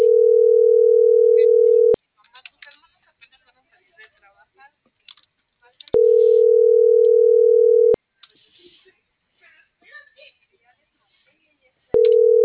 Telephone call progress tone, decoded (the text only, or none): ringback tone